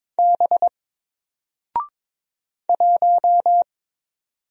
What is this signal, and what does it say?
Morse code, heard as a keyed tone, about 22 wpm, 700 Hz. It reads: B 1
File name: Single-Letter-Number-22wpm